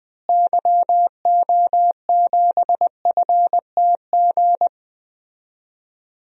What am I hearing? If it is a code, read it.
Morse code, heard as a keyed tone, about 20 wpm, 700 Hz. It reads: YO7FTG
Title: DXCC-100 Call Signs - 6 Characters - Repeated 3x - 20wpm